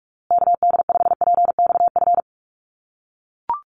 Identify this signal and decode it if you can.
Morse code, heard as a keyed tone, about 44 wpm, 700 Hz. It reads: KD5PXF
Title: USA Call Signs Repeated 3x 45wpm